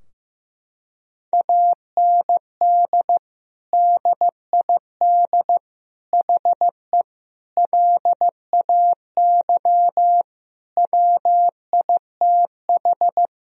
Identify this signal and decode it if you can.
Morse code, heard as a keyed tone, about 15 wpm, 700 Hz. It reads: AND DID HE LAY WITH
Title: Sentences from Top 1000 Words - Review Part 2 - 15wpm